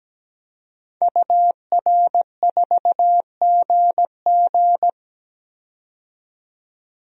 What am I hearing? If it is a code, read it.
Morse code, heard as a keyed tone, about 17 wpm, 700 Hz. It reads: UR4GG
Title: DXCC-100 Call Signs - 5 Characters - 17wpm